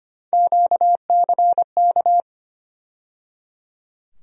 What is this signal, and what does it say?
Morse code, heard as a keyed tone, about 25 wpm, 700 Hz. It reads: QCK
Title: USA Call Sign Postfixes - 3 Letters 25wpm